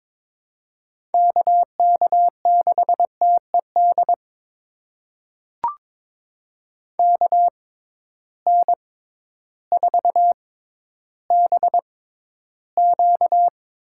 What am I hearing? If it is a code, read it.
Morse code, heard as a keyed tone, about 21 wpm, spaced as for 12 wpm, 700 Hz. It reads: KK6TED KN4BQ
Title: USA Call Signs - 6 Characters - 6x Character Spacing - 22wpm